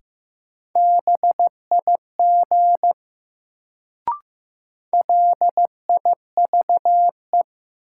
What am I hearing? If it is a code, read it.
Morse code, heard as a keyed tone, about 15 wpm, 700 Hz. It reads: BIG LIVE
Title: Sentences from Top 200 Words - New Word Drill - 15wpm